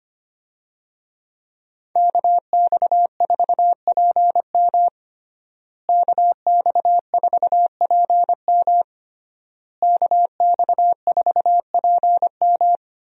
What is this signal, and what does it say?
Morse code, heard as a keyed tone, about 25 wpm, 700 Hz. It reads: KX4PM KX4PM KX4PM
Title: USA Call Signs - 5 Characters - Repeated 3x - 25wpm